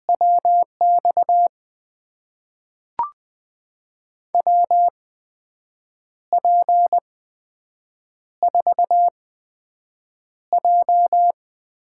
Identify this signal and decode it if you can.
Morse code, heard as a keyed tone, about 21 wpm, spaced as for 7 wpm, 700 Hz. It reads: WX WP4J
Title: USA Call Signs - 6 Characters - 8x Character Spacing - 20wpm